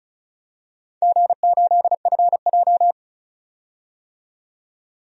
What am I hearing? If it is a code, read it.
Morse code, heard as a keyed tone, about 34 wpm, 700 Hz. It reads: G8FJ
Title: DXCC-100 Call Signs - 4 Characters - 35wpm